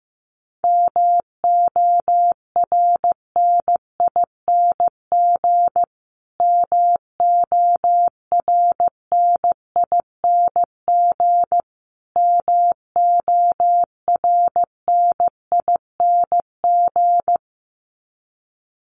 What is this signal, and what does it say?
Morse code, heard as a keyed tone, about 15 wpm, 700 Hz. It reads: MORNING MORNING MORNING
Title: Top 500 Words Repeated 15wpm